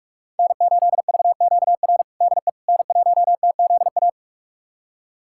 Morse code, heard as a keyed tone, about 46 wpm, 700 Hz. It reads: N8VQR DE N1TZA
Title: USA Callsign DE Callsign - 5 Characters - 45wpm